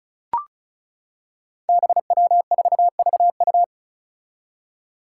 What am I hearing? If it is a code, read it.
Morse code, heard as a keyed tone, about 36 wpm, 700 Hz. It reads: BW4VU
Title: Five Challenging Letter-Number Combinations 35wpm